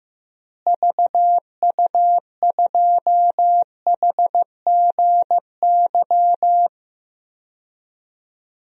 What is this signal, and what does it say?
Morse code, heard as a keyed tone, about 15 wpm, 700 Hz. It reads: VU2HGY
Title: DXCC-100 Call Signs - 6 Characters - Repeated 3x - 15wpm